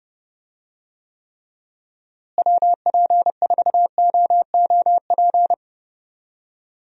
Morse code, heard as a keyed tone, about 30 wpm, 700 Hz. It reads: WP4OOP
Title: USA Call Signs - 6 Characters - Repeated 3x - 30wpm